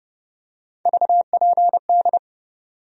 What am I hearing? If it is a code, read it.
Morse code, heard as a keyed tone, about 31 wpm, 700 Hz. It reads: VPD
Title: USA Call Sign Postfixes - 3 Letters 30wpm